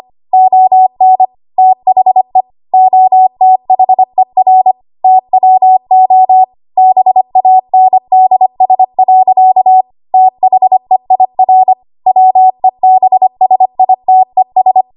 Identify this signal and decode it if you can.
Morse code, heard as a keyed tone, about 25 wpm, 750 Hz. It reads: ON THE OTHER TWO BANDS. THEIR WEBSITES
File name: Archive / 25 WPM / 25WPM